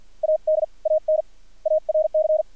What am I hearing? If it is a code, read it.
Morse code, heard as a keyed tone, about 47 wpm, 610 Hz. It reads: AD AN ARC
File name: archived / CRSnet